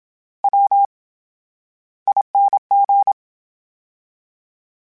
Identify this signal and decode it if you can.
Morse code, heard as a keyed tone, about 27 wpm, spaced as for 13 wpm, 800 Hz. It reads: WING
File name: Special INGs / Special INGs-mp3s / VerySlowToFaster Ing @25